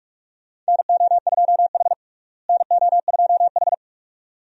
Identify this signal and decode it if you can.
Morse code, heard as a keyed tone, about 46 wpm, 700 Hz. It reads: NO2H NO2H
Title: USA Call Signs - 4 Characters - Repeated 2x - 45wpm